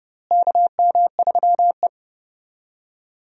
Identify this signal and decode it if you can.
Morse code, heard as a keyed tone, about 30 wpm, 700 Hz. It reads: KM3E
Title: USA Call Signs - 4 Characters - Repeated 3x - 30wpm